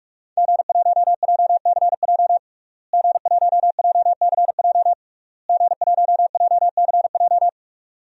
Morse code, heard as a keyed tone, about 46 wpm, 700 Hz. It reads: G1JCJ G1JCJ G1JCJ
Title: DXCC-100 Call Signs - 5 Characters - Repeated 3x - 45wpm